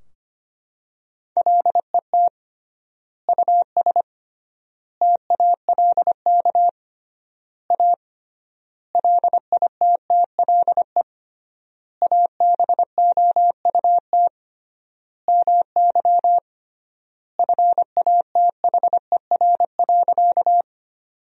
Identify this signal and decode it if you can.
Morse code, heard as a keyed tone, about 25 wpm, 700 Hz. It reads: LET US TALK A LITTLE ABOUT MY FATHER.
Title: 3x Word-Spacing - Sentences from Top 400 Words - 25wpm